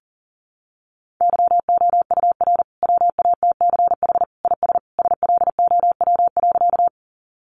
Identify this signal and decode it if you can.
Morse code, heard as a keyed tone, about 40 wpm, 700 Hz. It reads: YOUR WATCH IS SLOW.
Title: Sentences from Top 400 Words 40wpm